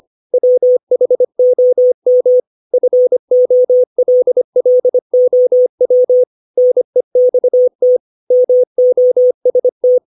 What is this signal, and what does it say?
Morse code, heard as a keyed tone, about 25 wpm, 500 Hz. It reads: WHOM FOLLOW NEXT MOST
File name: 632 words / 632 Words 25WPM